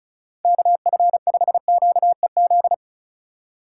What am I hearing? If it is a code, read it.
Morse code, heard as a keyed tone, about 43 wpm, 700 Hz. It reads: KF5QEZ
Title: Speed Racing - USA 2x3 Call Signs 35wpm